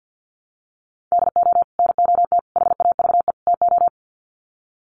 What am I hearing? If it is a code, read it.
Morse code, heard as a keyed tone, about 50 wpm, 700 Hz. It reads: DO NOT HAVE TO
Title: Sets of 4 Words 50wpm